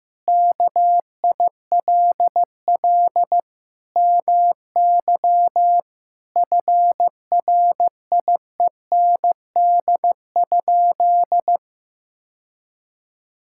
Morse code, heard as a keyed tone, about 15 wpm, 700 Hz. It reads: KILL MY FRIEND?
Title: Sentences from Top 1600 Words 15wpm